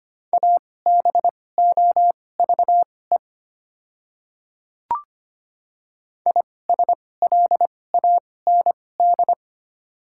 Morse code, heard as a keyed tone, about 25 wpm, spaced as for 18 wpm, 690 Hz. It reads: ABOVE ISLAND
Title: 2x Character Spacing - Top 500 Words - 25wpm